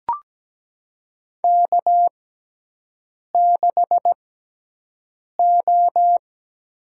Morse code, heard as a keyed tone, about 17 wpm, spaced as for 6 wpm, 700 Hz. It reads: K6O
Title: USA Call Signs - 6x Character Spacing - 17wpm